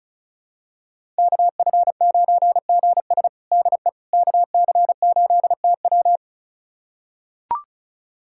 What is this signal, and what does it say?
Morse code, heard as a keyed tone, about 34 wpm, 700 Hz. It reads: KF9GS DE KC8TW
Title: USA Callsign DE Callsign - 5 Characters - 35wpm